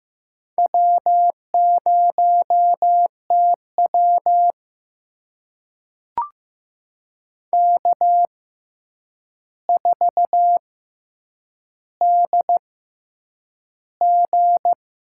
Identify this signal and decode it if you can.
Morse code, heard as a keyed tone, about 15 wpm, spaced as for 8 wpm, 700 Hz. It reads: W0TW K4DG
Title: USA Call Signs - 6x Character Spacing - 15wpm